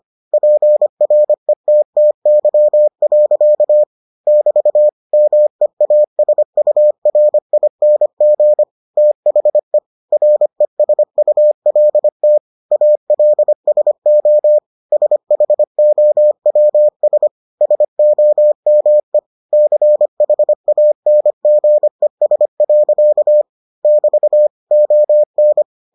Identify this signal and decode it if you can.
Morse code, heard as a keyed tone, about 25 wpm, 600 Hz. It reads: PRETTY. <BT> MEASURING THE RESULT ALSO SHOWS SOME CHANGES. <BT> ON